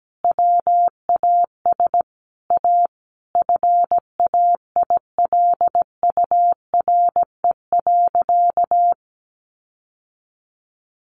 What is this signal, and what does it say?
Morse code, heard as a keyed tone, about 17 wpm, 700 Hz. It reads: WAS A FAILURE.
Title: Sentences from Top 1300 Words 17wpm